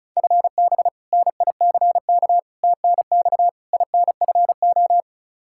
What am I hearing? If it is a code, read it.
Morse code, heard as a keyed tone, about 36 wpm, 700 Hz. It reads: FB NICK TNX INFO
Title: Practice QSO - Exchange 3 - More Info - 35wpm